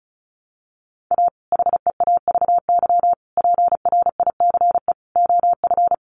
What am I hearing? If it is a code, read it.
Morse code, heard as a keyed tone, about 34 wpm, 700 Hz. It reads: A HEAVY PRICE OF
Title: Sentences from Top 1100 Words 35wpm